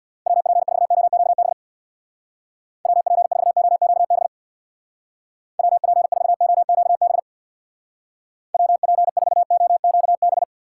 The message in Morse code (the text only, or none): WP4OQB WP4OQB WP4OQB WP4OQB